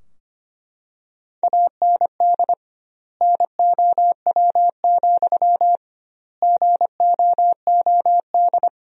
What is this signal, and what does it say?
Morse code, heard as a keyed tone, about 25 wpm, 700 Hz. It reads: AND NOW, GOOD
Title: 2x Word-Spacing - Sentences from Top 300 Words - 25wpm